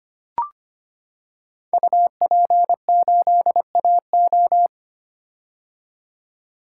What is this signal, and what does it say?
Morse code, heard as a keyed tone, about 25 wpm, 700 Hz. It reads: UP8AO
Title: Code Groups - 5 Letter:Numbers - 25wpm